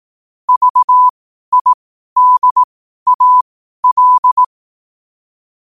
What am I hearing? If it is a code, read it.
Morse code, heard as a keyed tone, about 18 wpm, spaced as for 12 wpm, 1000 Hz. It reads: VIDAL
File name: qotdfiles / 12wpm